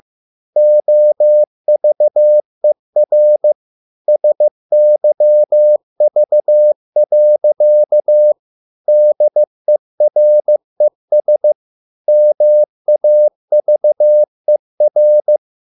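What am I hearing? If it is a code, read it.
Morse code, heard as a keyed tone, about 15 wpm, 600 Hz. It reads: OVER SYV. DERES MAVER
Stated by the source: Morse code at a slow 15 wpm from CW med Gnister.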